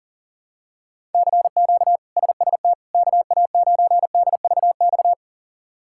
Morse code, heard as a keyed tone, about 40 wpm, 700 Hz. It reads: CQ SST KA9DVX
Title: K1USN SST QSO Practice - 40wpm